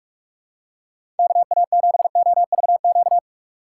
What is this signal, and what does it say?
Morse code, heard as a keyed tone, about 46 wpm, 700 Hz. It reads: KA7OVQ
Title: USA Call Signs - 6 Characters - 2x Character Spacing - 45wpm